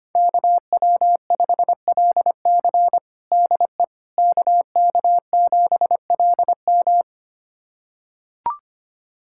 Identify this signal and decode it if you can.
Morse code, heard as a keyed tone, about 25 wpm, 700 Hz. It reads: KW5LC DE KK7LM
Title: USA Callsign DE Callsign - 5 Characters - 25wpm